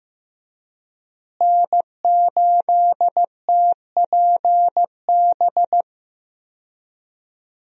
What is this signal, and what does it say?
Morse code, heard as a keyed tone, about 15 wpm, 700 Hz. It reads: N8TPB
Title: USA Call Signs - 5 Characters - Repeated 3x - 15wpm